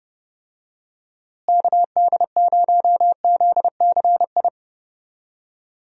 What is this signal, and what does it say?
Morse code, heard as a keyed tone, about 32 wpm, 700 Hz. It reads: KD0ZCI
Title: USA Call Signs - 6 Characters - Repeated 3x - 30wpm